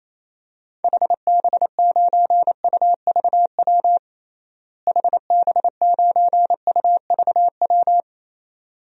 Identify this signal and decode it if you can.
Morse code, heard as a keyed tone, about 28 wpm, 700 Hz. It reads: HB9UVW HB9UVW
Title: DXCC-100 Call Signs - 6 Characters - Repeated 2x - 28wpm